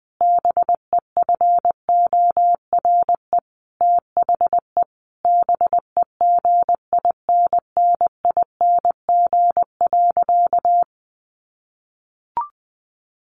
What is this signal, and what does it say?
Morse code, heard as a keyed tone, about 20 wpm, 700 Hz. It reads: BEFORE THE BEGINNING.
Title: A Sound Of Thunder 20wpm